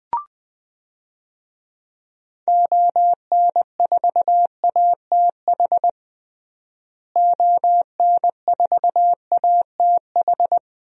Morse code, heard as a keyed tone, about 20 wpm, 700 Hz. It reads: ON4ATH ON4ATH
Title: DXCC-100 Call Signs - 6 Characters - Repeated 2x - 20wpm